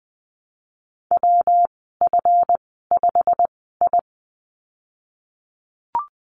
Morse code, heard as a keyed tone, about 20 wpm, spaced as for 15 wpm, 700 Hz. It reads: WF5I
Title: 173 Lesson 28g - Number 9 - Call Signs 20wpm